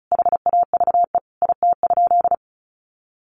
Morse code, heard as a keyed tone, about 34 wpm, 700 Hz. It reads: HAVE IT?
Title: Sentences from Top 100 Words 35wpm